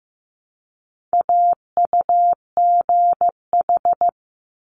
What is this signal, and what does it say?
Morse code, heard as a keyed tone, about 15 wpm, 700 Hz. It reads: AUGH
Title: Multi-letter Phonemes 15wpm